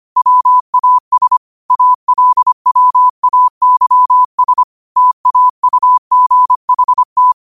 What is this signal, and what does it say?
Morse code, heard as a keyed tone, about 25 wpm, 1000 Hz. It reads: WAS ALWAYS TAUGHT
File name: qotdfiles / 25wpm